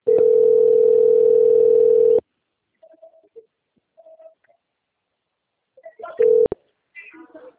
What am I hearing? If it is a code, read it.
Ringback tone (North America), 440 + 480 Hz.